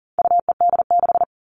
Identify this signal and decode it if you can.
Morse code, heard as a keyed tone, about 40 wpm, 700 Hz. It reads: UED6
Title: All Cut Numbers - 4-Digit 40wpm